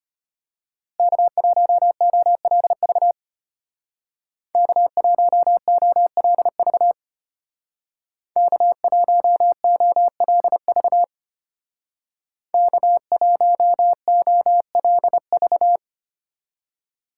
Morse code, heard as a keyed tone, about 32 wpm, 700 Hz. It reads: K1OLV K1OLV K1OLV K1OLV